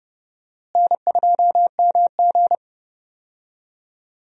Morse code, heard as a keyed tone, about 30 wpm, 700 Hz. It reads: N2MG
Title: USA Call Signs - 4 Characters - Repeated 3x - 30wpm